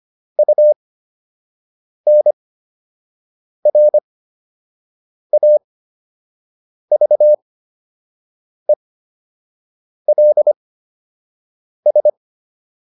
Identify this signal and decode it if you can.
Morse code, heard as a keyed tone, about 25 wpm, spaced as for 6 wpm, 600 Hz. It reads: UNRAVELS